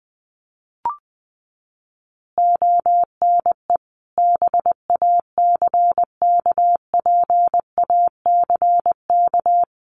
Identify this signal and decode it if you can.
Morse code, heard as a keyed tone, about 20 wpm, 700 Hz. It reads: ONE BACKPACK
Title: Additional Easy Phrases 20wpm